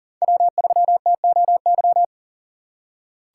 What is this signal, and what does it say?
Morse code, heard as a keyed tone, about 38 wpm, 700 Hz. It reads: W3TOY
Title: USA Call Signs - 5 Characters - 3x Character Spacing - 40wpm